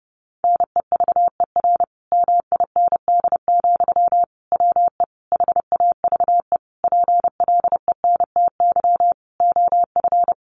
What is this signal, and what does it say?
Morse code, heard as a keyed tone, about 30 wpm, 700 Hz. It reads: NEVER MIND, WE HAVE PLENTY OF
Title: Sentences from Top 1300 Words 30wpm